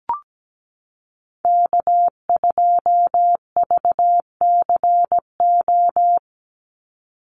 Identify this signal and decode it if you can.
Morse code, heard as a keyed tone, about 17 wpm, 700 Hz. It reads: K2VCO
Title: Call Signs - 17wpm